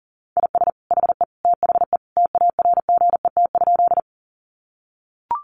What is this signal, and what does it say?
Morse code, heard as a keyed tone, about 40 wpm, 700 Hz. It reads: IS HE THE TARGET?
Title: Sentences from Top 1200 Words 40wpm